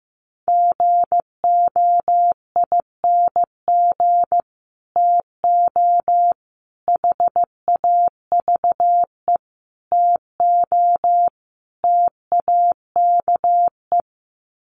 Morse code, heard as a keyed tone, about 14 wpm, 700 Hz. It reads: GOING TO HAVE TO TAKE
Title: Sets of 5 Words Encore 15wpm